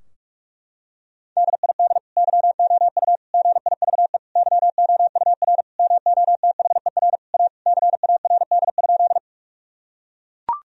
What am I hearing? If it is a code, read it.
Morse code, heard as a keyed tone, about 41 wpm, 700 Hz. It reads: DID YOU GIVE YOUR MOTHER A CARD?
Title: Sentences from Top 1000 Words - Review Part 4 - 45wpm